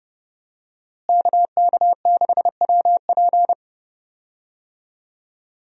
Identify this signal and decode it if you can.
Morse code, heard as a keyed tone, about 29 wpm, 700 Hz. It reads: KK6WP
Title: USA Call Signs - 5 Characters - Repeated 3x - 30wpm